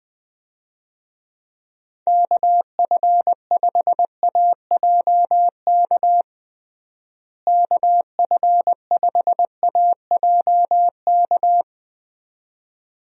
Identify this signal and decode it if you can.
Morse code, heard as a keyed tone, about 21 wpm, 700 Hz. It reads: KF5AJK KF5AJK
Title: USA Call Signs - 6 Characters - Repeated 2x - 20wpm